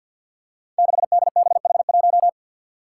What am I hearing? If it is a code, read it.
Morse code, heard as a keyed tone, about 53 wpm, 700 Hz. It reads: 6DBH1
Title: Five Challenging Letter-Number Combinations 50wpm